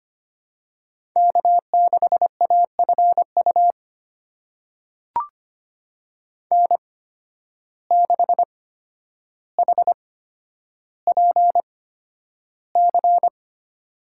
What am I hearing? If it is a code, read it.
Morse code, heard as a keyed tone, about 25 wpm, spaced as for 11 wpm, 700 Hz. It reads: K6AFU N6HPC
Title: USA Call Signs - 8x Character Spacing - 25wpm